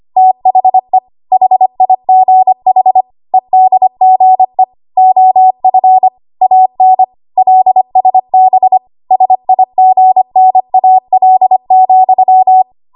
Morse code, heard as a keyed tone, about 25 wpm, 750 Hz. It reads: THE HIGH EDGE OF AN LSB SIGNAL,
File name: Archive / 25 WPM / 25WPM